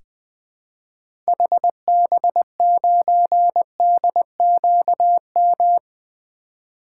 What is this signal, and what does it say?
Morse code, heard as a keyed tone, about 20 wpm, 700 Hz. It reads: HB9DQM
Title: 2023 SOTA Activator Call Signs - 20wpm